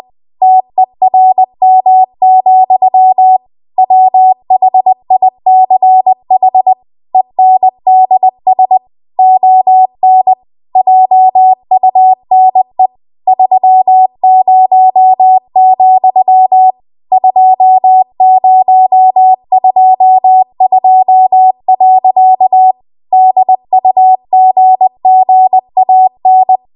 Morse code, heard as a keyed tone, about 20 wpm, 750 Hz. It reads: TERM, WHICH ENDS ON JUNE 30, 2022. DUGGAN
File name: Archive / 20 WPM / 20WPM